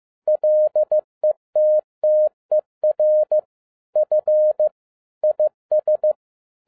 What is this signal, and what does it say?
Morse code, heard as a keyed tone, about 15 wpm, 610 Hz. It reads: LETTER F IS
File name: 2011-10-mp3 / n07-A DOT 15wpm.mp3